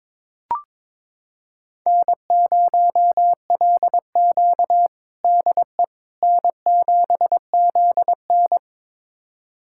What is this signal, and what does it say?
Morse code, heard as a keyed tone, about 21 wpm, 700 Hz. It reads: N0LQ DE N7ZN
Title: USA Callsign DE Callsign - 4 Characters - 22wpm